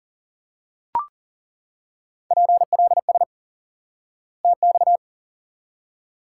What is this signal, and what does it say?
Morse code, heard as a keyed tone, about 40 wpm, 700 Hz. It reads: PLS TX
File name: IQE Course 3 - Lesson 89 - YAGI - Two Elements - 40wpm